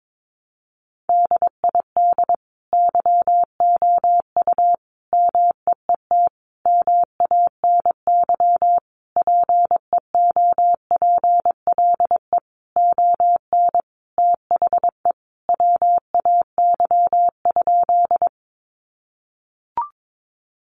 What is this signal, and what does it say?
Morse code, heard as a keyed tone, about 22 wpm, 700 Hz. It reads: DID YOU MEET MANY PEOPLE ON THE WAY?
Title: Sentences from Top 1100 Words 22wpm